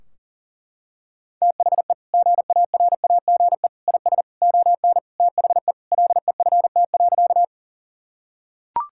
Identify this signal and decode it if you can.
Morse code, heard as a keyed tone, about 40 wpm, 700 Hz. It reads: THE GARAGE IS ON THE LEFT.
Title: Sentences from Top 1600 Words 40wpm